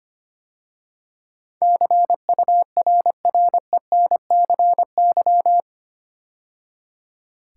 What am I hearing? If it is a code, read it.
Morse code, heard as a keyed tone, about 25 wpm, 700 Hz. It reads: CURRENCY
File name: 8 Letter Words - First 3 Letters Given - 25wpm